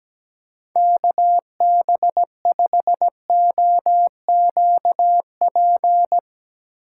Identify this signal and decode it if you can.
Morse code, heard as a keyed tone, about 17 wpm, 700 Hz. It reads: KB5OQP
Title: USA Call Signs - 6 Characters - 3x Character Spacing - 17wpm